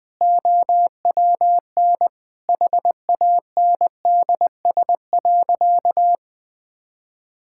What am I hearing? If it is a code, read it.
Morse code, heard as a keyed tone, about 20 wpm, 700 Hz. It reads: OWN HANDS.
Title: Sentences with 7-Word Phrases - 20wpm